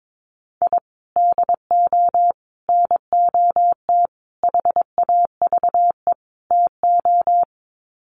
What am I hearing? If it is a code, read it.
Morse code, heard as a keyed tone, about 22 wpm, 700 Hz. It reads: I DO NOT HAVE TO
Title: Sets of 5 Words 22wpm